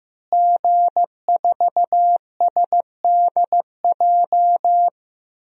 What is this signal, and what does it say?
Morse code, heard as a keyed tone, about 15 wpm, 700 Hz. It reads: G4SDJ
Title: DXCC-100 Call Signs - 5 Characters - Repeated 2x - 15wpm